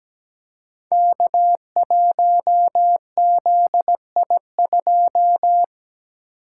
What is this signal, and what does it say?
Morse code, heard as a keyed tone, about 17 wpm, 700 Hz. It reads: K1ZI2
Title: Code Groups - 5 Letter:Numbers - 17wpm